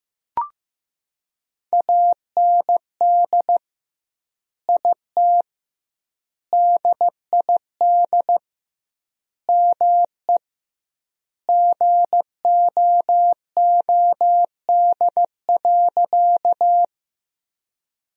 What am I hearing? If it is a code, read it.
Morse code, heard as a keyed tone, about 15 wpm, 700 Hz. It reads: AND IT DID ME GOOD.
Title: 2x Word-Spacing - Sentences from Top 200 Words - 15wpm